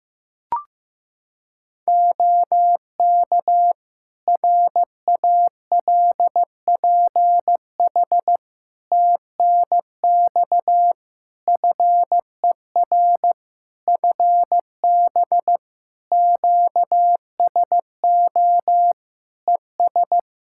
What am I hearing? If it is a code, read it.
Morse code, heard as a keyed tone, about 16 wpm, 700 Hz. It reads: OK RALPH TNX FER FB QSO ES
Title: Practice QSO - Exchange 4 - Ending - 15wpm